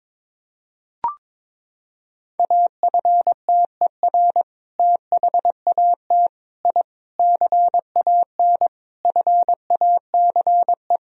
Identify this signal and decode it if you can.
Morse code, heard as a keyed tone, about 22 wpm, 700 Hz. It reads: AFTER THAT I CAN FACE
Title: Sentences from Top 1000 Words - Review Part 2 - 22wpm